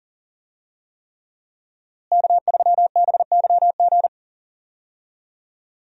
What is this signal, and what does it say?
Morse code, heard as a keyed tone, about 40 wpm, 700 Hz. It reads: K3BYG
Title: USA Call Signs - 5 Characters - Repeated 3x - 40wpm